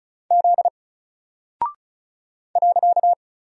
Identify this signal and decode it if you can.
Morse code, heard as a keyed tone, about 36 wpm, 700 Hz. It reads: Z .
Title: Single Character (Letter:Number:Punctuation:Prosign) 35wpm